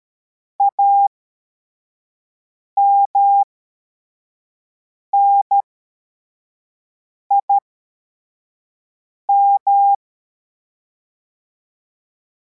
Morse code, heard as a keyed tone, about 13 wpm, spaced as for 4 wpm, 800 Hz. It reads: AMNIM
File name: formation / cw / Text33